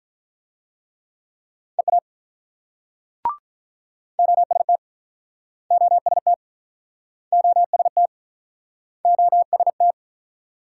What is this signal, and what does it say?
Morse code, heard as a keyed tone, about 46 wpm, 700 Hz. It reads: EA OST OST OST OST